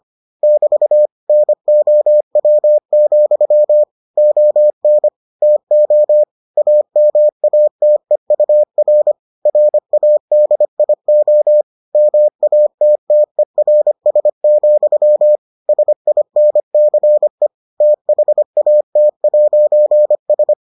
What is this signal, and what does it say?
Morse code, heard as a keyed tone, about 25 wpm, 600 Hz. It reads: <BT> NOW, ON TO AMATEUR RADIO MATTERS, SINCE THAT'S